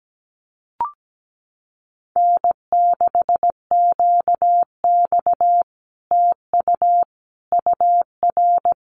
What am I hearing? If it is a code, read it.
Morse code, heard as a keyed tone, about 17 wpm, 700 Hz. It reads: N6QX TU UR
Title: Common QSO Phrases Encore 17wpm